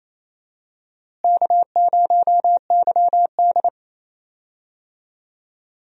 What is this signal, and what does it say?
Morse code, heard as a keyed tone, about 28 wpm, 700 Hz. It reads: K0YD
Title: USA Call Signs - 4 Characters - Repeated 3x - 28wpm